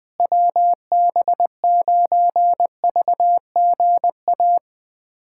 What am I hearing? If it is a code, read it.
Morse code, heard as a keyed tone, about 20 wpm, 700 Hz. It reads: WB9VGA
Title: USA Callsign DE Callsign - 6 Characters - 20wpm